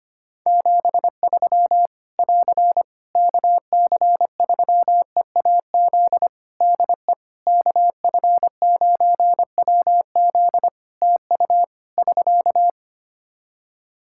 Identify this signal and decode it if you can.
Morse code, heard as a keyed tone, about 25 wpm, 700 Hz. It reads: 73 <AR> KC3EAZ DE KF9WZ TU <SK>
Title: Practice QSO - Exchange 4 - Ending - 25wpm